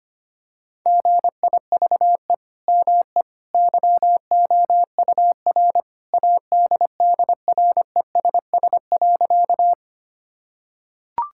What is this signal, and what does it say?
Morse code, heard as a keyed tone, about 25 wpm, 700 Hz. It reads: GIVE ME YOUR ADDRESS.
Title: Sentences from Top 1000 Words - Review Part 6 - 25wpm